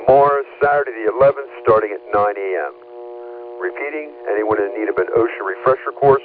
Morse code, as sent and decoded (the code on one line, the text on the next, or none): none